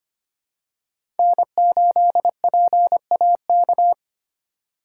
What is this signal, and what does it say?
Morse code, heard as a keyed tone, about 25 wpm, 700 Hz. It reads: N8PAK
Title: USA Call Signs - 4x Character Spacing - 25wpm